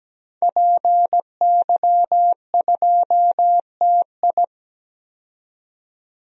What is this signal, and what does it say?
Morse code, heard as a keyed tone, about 17 wpm, 700 Hz. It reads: PY2TI
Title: DXCC-100 Call Signs - 5 Characters - 17wpm